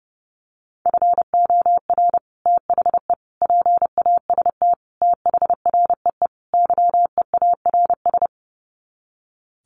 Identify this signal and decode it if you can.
Morse code, heard as a keyed tone, about 30 wpm, 700 Hz. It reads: FOR THE PAST THREE YEARS
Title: Sets of 5 Words Encore2 30wpm